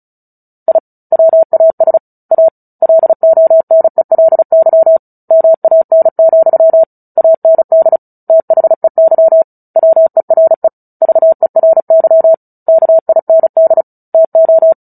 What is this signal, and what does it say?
Morse code, heard as a keyed tone, about 34 wpm, 650 Hz. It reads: I WAS A LONELY MAN, AND THEY WERE VERY KIND TO
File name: Book Two / 09-Chapter 9 - 35 WPM